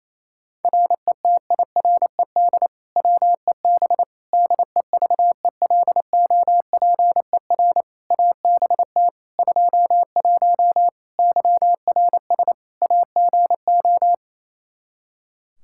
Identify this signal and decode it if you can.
Morse code, heard as a keyed tone, about 28 wpm, 700 Hz. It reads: RETIRED WEB DEVELOPER ABT 21 YRS AGO
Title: Practice QSO - Exchange 3 - More Info - 28wpm